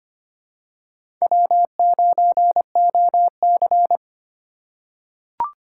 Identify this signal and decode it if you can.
Morse code, heard as a keyed tone, about 25 wpm, 700 Hz. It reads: W9OC
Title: USA Call Signs - 4 Characters - 4x Character Spacing - 25wpm